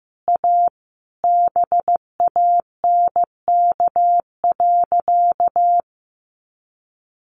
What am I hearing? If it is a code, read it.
Morse code, heard as a keyed tone, about 15 wpm, 700 Hz. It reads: A BANK.
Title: Sentences from Top 1500 Words 15wpm